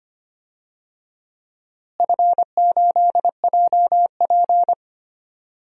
Morse code, heard as a keyed tone, about 25 wpm, 700 Hz. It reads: F8JP
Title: DXCC-100 Call Signs - 4 Characters - Repeated 2x - 25wpm